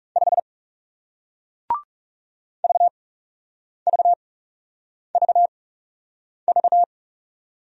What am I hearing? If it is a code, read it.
Morse code, heard as a keyed tone, about 40 wpm, spaced as for 8 wpm, 700 Hz. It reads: 5 VVVV